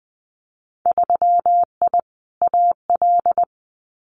Morse code, heard as a keyed tone, about 20 wpm, 700 Hz. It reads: 3I AL
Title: Winter Field Day Exchanges 20wpm